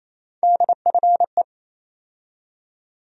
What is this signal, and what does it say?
Morse code, heard as a keyed tone, about 27 wpm, 700 Hz. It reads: DFE
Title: USA Call Sign Postfixes - 3 Letters 28wpm